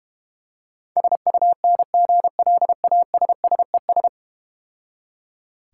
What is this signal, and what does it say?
Morse code, heard as a keyed tone, about 32 wpm, 700 Hz. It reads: SUNGLASSES